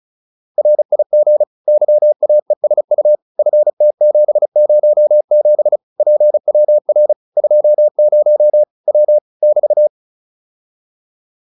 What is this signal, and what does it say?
Morse code, heard as a keyed tone, about 34 wpm, 600 Hz. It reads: RIG YAESU FT707 PWR 20 W <BT>
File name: QSO403 35